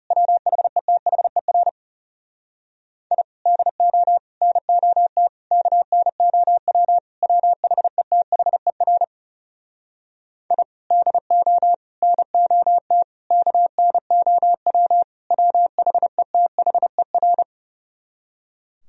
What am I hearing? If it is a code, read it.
Morse code, heard as a keyed tone, about 40 wpm, 700 Hz. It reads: WHETHER I DO NOT KNOW WHETHER I DO NOT KNOW WHETHER